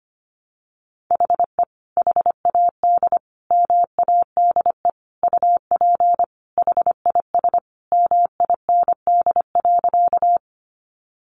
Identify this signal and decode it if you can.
Morse code, heard as a keyed tone, about 25 wpm, 700 Hz. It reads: HE HAD MADE UP HIS MIND.
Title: Sentences from Top 500 Words 25wpm